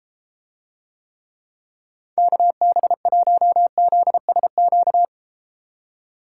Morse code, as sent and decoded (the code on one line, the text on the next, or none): -.- -... .---- --.. ... --.-
KB1ZSQ